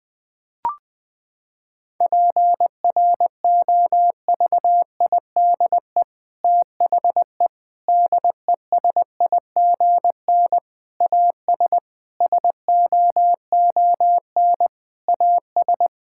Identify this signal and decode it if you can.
Morse code, heard as a keyed tone, about 21 wpm, 700 Hz. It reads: PROVIDE THE DESIGN AS SOON AS
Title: Sentences from Top 1000 Words - Review Part 5 - 20wpm